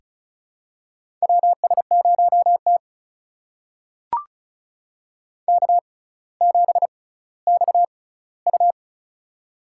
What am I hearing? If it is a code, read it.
Morse code, heard as a keyed tone, about 36 wpm, spaced as for 18 wpm, 700 Hz. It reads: WS0T K7XU
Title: USA Call Signs - 4 Characters - 6x Character Spacing - 35wpm